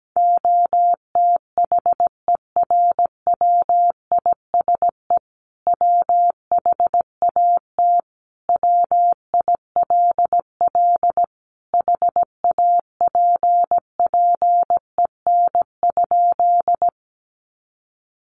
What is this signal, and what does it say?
Morse code, heard as a keyed tone, about 17 wpm, 700 Hz. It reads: OTHERWISE WHAT WILL HAPPEN?
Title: Sentences from Top 1200 Words 17wpm